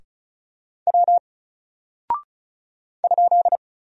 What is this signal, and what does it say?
Morse code, heard as a keyed tone, about 36 wpm, 700 Hz. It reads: W ?
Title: Single Character (Letter:Number:Punctuation:Prosign) 35wpm